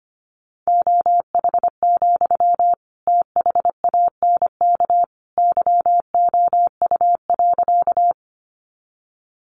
Morse code, heard as a keyed tone, about 25 wpm, 700 Hz. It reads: OH, THANK YOU.
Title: Sentences from Top 600 Words 25wpm